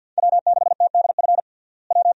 Morse code, heard as a keyed tone, about 52 wpm, 700 Hz. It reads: W6TDF W
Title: USA Call Signs - 5 Characters - Repeated 3x - 50wpm